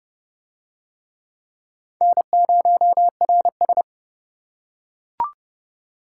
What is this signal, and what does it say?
Morse code, heard as a keyed tone, about 29 wpm, 700 Hz. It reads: N0RS